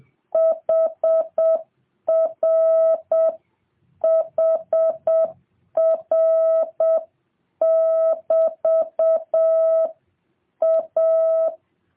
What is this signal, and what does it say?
Morse code, heard as a keyed tone, about 7 wpm, 650 Hz. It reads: HR HR <BT> A